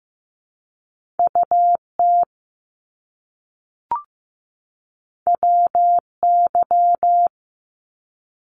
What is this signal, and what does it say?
Morse code, heard as a keyed tone, about 15 wpm, 700 Hz. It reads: UT WY
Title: ARRL Sections 15wpm